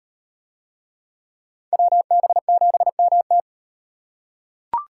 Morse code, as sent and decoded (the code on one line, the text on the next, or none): .-- -... --... -- -
WB7MT